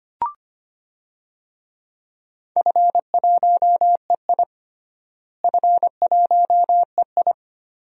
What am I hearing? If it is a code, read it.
Morse code, heard as a keyed tone, about 25 wpm, 700 Hz. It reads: F1EI F1EI
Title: DXCC-100 Call Signs - 4 Characters - Repeated 2x - 25wpm